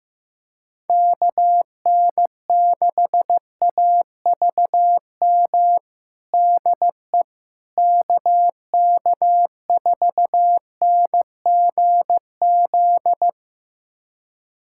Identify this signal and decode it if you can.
Morse code, heard as a keyed tone, about 15 wpm, 700 Hz. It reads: KN6AVM DE KK4NGZ
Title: USA Callsign DE Callsign 15wpm